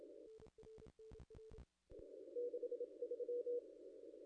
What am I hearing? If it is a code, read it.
Morse code, heard as a keyed tone, about 27 wpm, 480 Hz. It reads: 63
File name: contests / ss-cw-k5tr-op-n5rz -0100